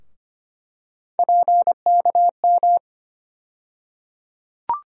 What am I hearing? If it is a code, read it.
Morse code, heard as a keyed tone, about 25 wpm, 700 Hz. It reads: PKM
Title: USA Call Sign Postfixes - 3 Letters 25wpm